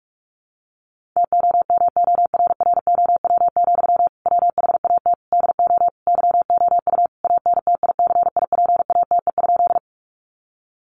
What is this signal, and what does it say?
Morse code, heard as a keyed tone, about 44 wpm, 700 Hz. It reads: TOMORROW, WHAT DO YOU ANTICIPATE?
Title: Sentences from Top 1500 Words 45wpm